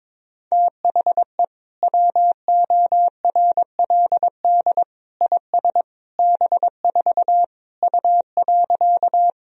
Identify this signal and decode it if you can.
Morse code, heard as a keyed tone, about 22 wpm, 700 Hz. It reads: THE WORLD IS B4 U.
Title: Sentences from Top 200 Words and Abbreviations 22wpm